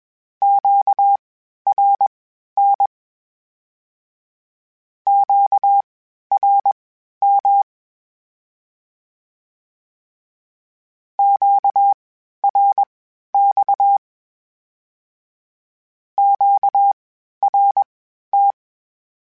Morse code, heard as a keyed tone, about 21 wpm, spaced as for 12 wpm, 800 Hz. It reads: QRN QRM QRX QRT
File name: (spaced) / Q-Calls Spaced-mp3s / Q Calls sp Once @20